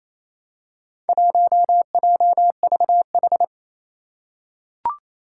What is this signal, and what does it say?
Morse code, heard as a keyed tone, about 28 wpm, 700 Hz. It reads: 1JVH
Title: Four Challenging Letter-Number Combinations 28wpm